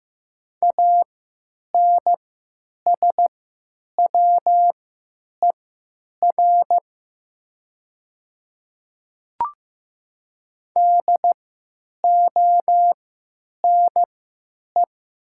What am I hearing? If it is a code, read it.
Morse code, heard as a keyed tone, about 15 wpm, spaced as for 9 wpm, 700 Hz. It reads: ANSWER DONE
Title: 3x Character Spacing - Top 500 Words - 15wpm